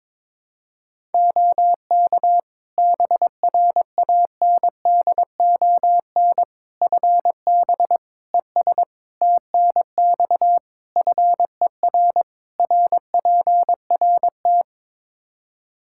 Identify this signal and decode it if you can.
Morse code, heard as a keyed tone, about 22 wpm, 700 Hz. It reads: OK BRANDON FB ES TNX FER RPRT
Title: K1USN SST QSO Practice 22wpm